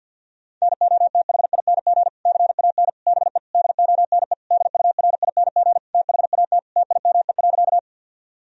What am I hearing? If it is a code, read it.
Morse code, heard as a keyed tone, about 51 wpm, 700 Hz. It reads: NOTHING CAN BE DONE DURING THAT TIME.
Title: Sentences from Top 1000 Words - Review Part 2 - 50wpm